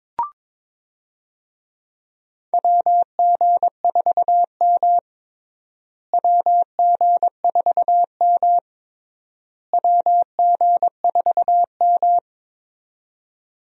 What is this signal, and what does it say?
Morse code, heard as a keyed tone, about 21 wpm, 700 Hz. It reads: WG4M WG4M WG4M
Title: USA Call Signs - 4 Characters - Repeated 3x - 22wpm